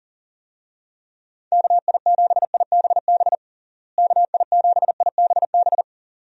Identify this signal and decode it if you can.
Morse code, heard as a keyed tone, about 40 wpm, 700 Hz. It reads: KI7IBB KI7IBB
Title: USA Call Signs - 6 Characters - Repeated 2x - 40wpm